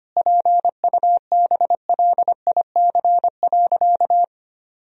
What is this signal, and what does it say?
Morse code, heard as a keyed tone, about 25 wpm, 700 Hz. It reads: PUBLIC.
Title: Sentences from Top 1000 Words - Review Part 3 - 25wpm